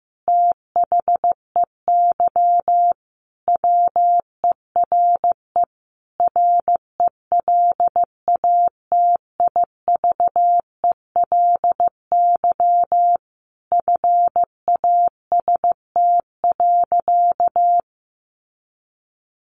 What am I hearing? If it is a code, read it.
Morse code, heard as a keyed tone, about 15 wpm, 700 Hz. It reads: THEY WERE RELATIVELY FAST.
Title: Sentences from Top 1100 Words 15wpm